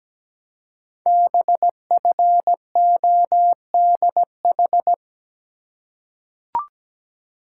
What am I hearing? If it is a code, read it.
Morse code, heard as a keyed tone, about 17 wpm, 700 Hz. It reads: BFODH
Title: Code Groups - 5 Letters - 17wpm